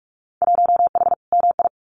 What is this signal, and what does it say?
Morse code, heard as a keyed tone, about 46 wpm, 700 Hz. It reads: 1H MI
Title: Winter Field Day Exchanges 45wpm